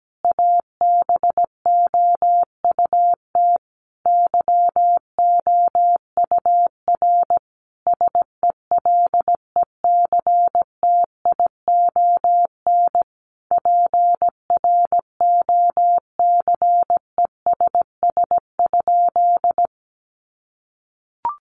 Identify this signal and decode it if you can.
Morse code, heard as a keyed tone, about 17 wpm, 700 Hz. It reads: ABOUT YOUR SELECTION PROCESS?
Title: Sentences from Top 1400 Words 17wpm